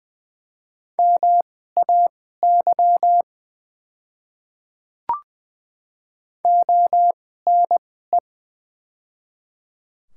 Morse code, heard as a keyed tone, about 20 wpm, spaced as for 15 wpm, 700 Hz. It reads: MAY ONE
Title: 2x Character Spacing - Top 200 Words - 20wpm